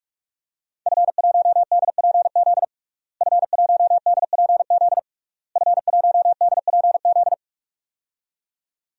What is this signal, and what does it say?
Morse code, heard as a keyed tone, about 46 wpm, 700 Hz. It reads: F1DPZ F1DPZ F1DPZ
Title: DXCC-100 Call Signs - 5 Characters - Repeated 3x - 45wpm